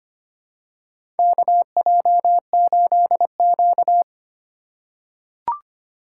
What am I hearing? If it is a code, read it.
Morse code, heard as a keyed tone, about 25 wpm, 700 Hz. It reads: KJ8Q
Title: USA Call Signs - 2x Character Spacing - 25wpm